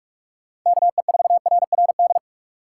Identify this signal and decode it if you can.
Morse code, heard as a keyed tone, about 44 wpm, 700 Hz. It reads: KE4RRD